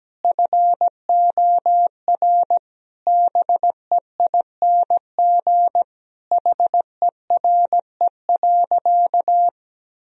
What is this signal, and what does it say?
Morse code, heard as a keyed tone, about 17 wpm, 700 Hz. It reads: FOR BEING HERE.
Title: Sentences from Top 1000 Words - Review Part 3 - 17wpm